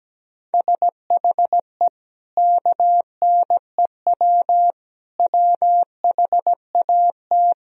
Morse code, heard as a keyed tone, about 17 wpm, 700 Hz. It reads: SHE KNEW WHAT
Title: Sentences from Top 400 Words and Abbreviations 17wpm